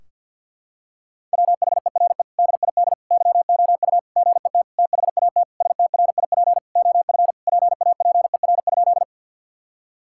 Morse code, heard as a keyed tone, about 50 wpm, 700 Hz. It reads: WHERE DID YOU GET THAT STRIP OF PAPER?
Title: Sentences from Top 1600 Words 50wpm